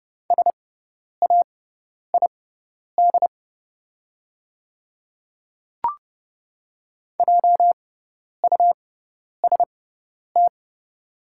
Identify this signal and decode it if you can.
Morse code, heard as a keyed tone, about 30 wpm, spaced as for 10 wpm, 690 Hz. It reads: SAID JUST
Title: 6x Character Spacing - Top 200 Words - 30wpm